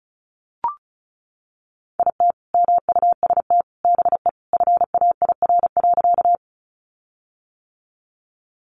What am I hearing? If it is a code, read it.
Morse code, heard as a keyed tone, about 34 wpm, 700 Hz. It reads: IT MUST BE FAIR.
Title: Sentences from Top 1300 Words 35wpm